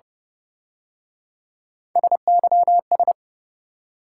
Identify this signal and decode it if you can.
Morse code, heard as a keyed tone, about 29 wpm, 700 Hz. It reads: SYS
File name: Speed-Racing - Three Letter-Number Combinations - 25wpm